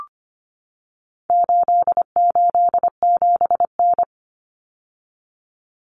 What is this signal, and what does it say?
Morse code, heard as a keyed tone, about 25 wpm, 700 Hz. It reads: 887N
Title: Common Cut Numbers - 4-Digit 25wpm